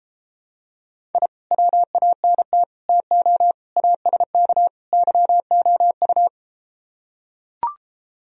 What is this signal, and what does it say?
Morse code, heard as a keyed tone, about 33 wpm, 700 Hz. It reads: I WANT TO ASK YOU